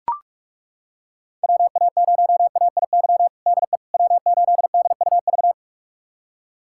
Morse code, heard as a keyed tone, about 42 wpm, 700 Hz. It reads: WA0AIY DE W8DUV
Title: USA Callsign DE Callsign 45wpm